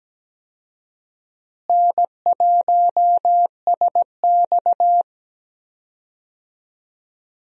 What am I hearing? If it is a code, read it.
Morse code, heard as a keyed tone, about 17 wpm, 700 Hz. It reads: N1SX
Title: USA Call Signs - 4 Characters - Repeated 3x - 17wpm